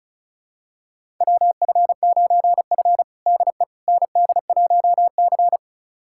Morse code, heard as a keyed tone, about 36 wpm, 700 Hz. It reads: WF9F DE ND1C
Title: USA Callsign DE Callsign - 4 Characters - 35wpm